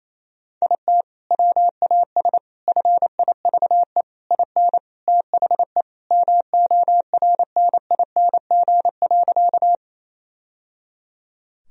Morse code, heard as a keyed tone, about 28 wpm, 700 Hz. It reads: IT WAS FIVE IN THE MORNING.
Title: Sentences from Top 1000 Words - Review Part 2 - 28wpm